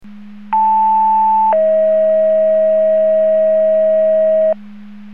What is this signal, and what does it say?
Two-tone page, heard as a one-second tone, then a three-second tone, about 880 then 640 Hz, with noise. It is heard from 0.5 to 4.5 s.